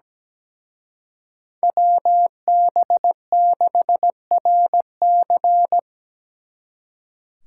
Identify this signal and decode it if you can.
Morse code, heard as a keyed tone, about 20 wpm, 700 Hz. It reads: WB6RC
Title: Speed Racing - USA Call Signs 17wpm